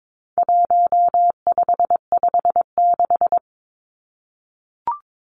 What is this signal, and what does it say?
Morse code, heard as a keyed tone, about 22 wpm, 700 Hz. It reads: 1556
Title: Numbers - 4-Digit 22wpm